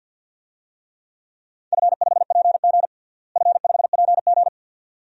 Morse code, heard as a keyed tone, about 47 wpm, 700 Hz. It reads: F5PG F5PG
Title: DXCC-100 Call Signs - 4 Characters - Repeated 2x - 50wpm